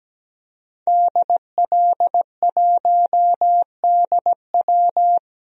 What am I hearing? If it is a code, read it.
Morse code, heard as a keyed tone, about 17 wpm, 700 Hz. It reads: DL1DW
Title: DXCC-100 Call Signs - 5 Characters - 17wpm